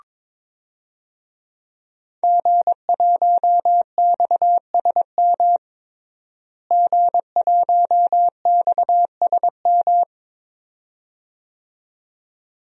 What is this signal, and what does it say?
Morse code, heard as a keyed tone, about 22 wpm, 700 Hz. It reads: G1XSM G1XSM
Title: DXCC-100 Call Signs - 5 Characters - Repeated 2x - 22wpm